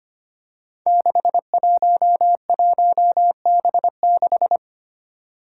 Morse code, heard as a keyed tone, about 25 wpm, 700 Hz. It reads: 611B6
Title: Five Challenging Letter-Number Combinations 25wpm